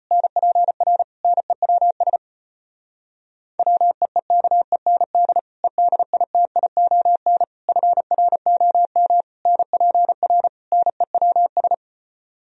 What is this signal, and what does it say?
Morse code, heard as a keyed tone, about 33 wpm, 700 Hz. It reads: NPR NEWS WEEKEND EDITION FROM NPR NEWS